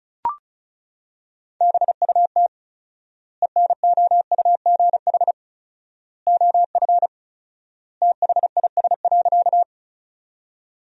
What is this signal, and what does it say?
Morse code, heard as a keyed tone, about 36 wpm, 700 Hz. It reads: BUT ENOUGH OF THIS.
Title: 4x Word-Spacing - Sentences from Top 400 Words - 35wpm